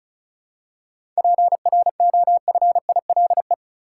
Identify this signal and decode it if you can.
Morse code, heard as a keyed tone, about 36 wpm, 700 Hz. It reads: PROFILE
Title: Seven Letter Words - 35wpm